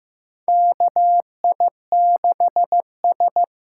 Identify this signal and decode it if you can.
Morse code, heard as a keyed tone, about 15 wpm, 700 Hz. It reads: KI6S
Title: USA Call Signs - 6 Characters - 8x Character Spacing - 15wpm